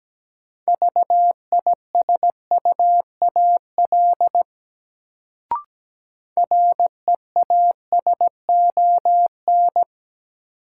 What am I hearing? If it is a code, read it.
Morse code, heard as a keyed tone, about 17 wpm, 700 Hz. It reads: VISUAL REASON
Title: Sets of 5 Words Encore - New Word Drill - 17wpm